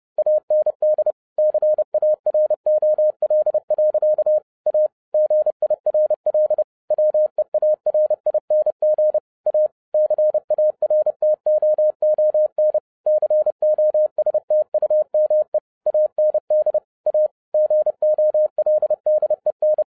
Morse code, heard as a keyed tone, about 30 wpm, 610 Hz. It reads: AND CAROL. A GIRL WEARING A CARTOON COSTUME AND A GOLDEN